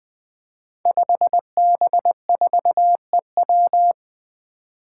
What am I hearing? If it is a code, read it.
Morse code, heard as a keyed tone, about 20 wpm, 700 Hz. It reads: 5B4EW
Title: Five Challenging Letter-Number Combinations 20wpm